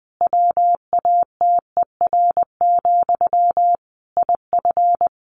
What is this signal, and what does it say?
Morse code, heard as a keyed tone, about 20 wpm, 700 Hz. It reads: WATER, IF
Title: Sentences from Top 1500 Words 20wpm